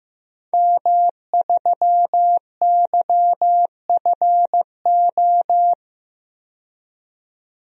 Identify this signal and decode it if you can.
Morse code, heard as a keyed tone, about 15 wpm, 700 Hz. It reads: M3YFO
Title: DXCC-100 Call Signs - 5 Characters - 15wpm